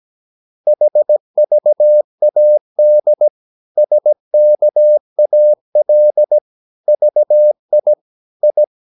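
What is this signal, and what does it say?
Morse code, heard as a keyed tone, about 17 wpm, 600 Hz. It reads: HVAD SKAL VI I
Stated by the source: Slow (QRS) Morse practice at 17 wpm, from CW med Gnister.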